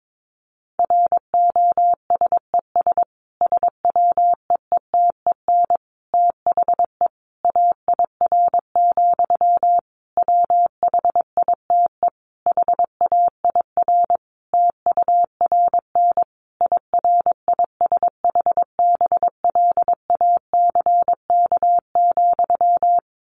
Morse code, heard as a keyed tone, about 22 wpm, 700 Hz. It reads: ROSES SWEETEN THE AIR, WHITE HAIR TURN IRISHBLACK,
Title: A Sound Of Thunder 22wpm